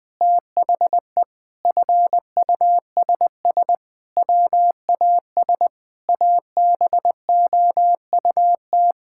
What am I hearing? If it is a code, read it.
Morse code, heard as a keyed tone, about 20 wpm, 700 Hz. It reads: THE FUSS WAS ABOUT
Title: Sets of 8 Words - 20wpm